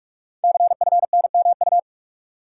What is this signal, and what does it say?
Morse code, heard as a keyed tone, about 44 wpm, 690 Hz. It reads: CFNMU
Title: Code Groups - 5 Letters - 45wpm